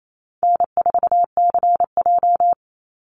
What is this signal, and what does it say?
Morse code, heard as a keyed tone, about 28 wpm, 700 Hz. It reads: N4CJ
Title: Call Signs - 28wpm